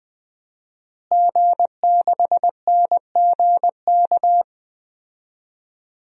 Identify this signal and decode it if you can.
Morse code, heard as a keyed tone, about 20 wpm, 700 Hz. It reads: G6NGK
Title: DXCC-100 Call Signs - 5 Characters - Repeated 3x - 20wpm